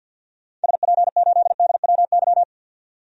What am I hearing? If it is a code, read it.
Morse code, heard as a keyed tone, about 52 wpm, 700 Hz. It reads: SP8DWY
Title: DXCC-100 Call Signs - 6 Characters - 50wpm